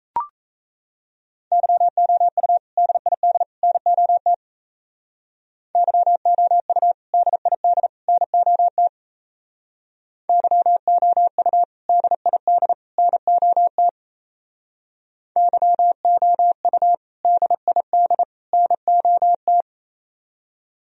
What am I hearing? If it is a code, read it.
Morse code, heard as a keyed tone, about 35 wpm, 700 Hz. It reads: YOU DID NOT YOU DID NOT YOU DID NOT YOU DID NOT
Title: Speed-Racing - Sets of 3 Words 28wpm